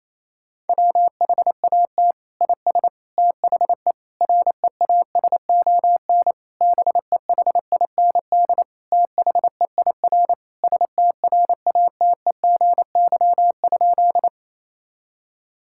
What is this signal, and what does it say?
Morse code, heard as a keyed tone, about 28 wpm, 700 Hz. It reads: WHAT IS THE REASON BEHIND THEIR STRATEGY?
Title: Sentences from Top 1000 Words - Review Part 5 - 28wpm